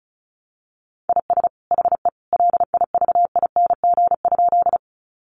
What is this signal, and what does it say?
Morse code, heard as a keyed tone, about 34 wpm, 700 Hz. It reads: IS HE LIVING?
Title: Sentences from Top 1100 Words 35wpm